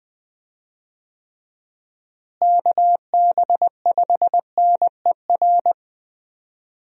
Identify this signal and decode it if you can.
Morse code, heard as a keyed tone, about 20 wpm, 700 Hz. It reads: KB5NER
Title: USA Call Signs - 6 Characters - Repeated 2x - 20wpm